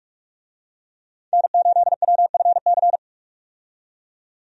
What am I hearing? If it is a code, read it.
Morse code, heard as a keyed tone, about 46 wpm, 700 Hz. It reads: N8WFC
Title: USA Call Signs - 5 Characters - 45wpm